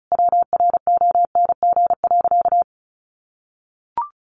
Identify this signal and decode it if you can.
Morse code, heard as a keyed tone, about 34 wpm, 700 Hz. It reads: WRONG.
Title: Sentences from Top 1200 Words 35wpm